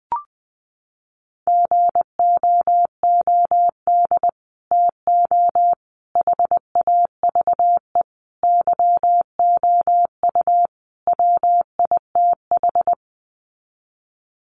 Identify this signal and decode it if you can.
Morse code, heard as a keyed tone, about 19 wpm, 700 Hz. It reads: GOOD TO HAVE YOU WITH
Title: Sets of 5 Words Encore2 20wpm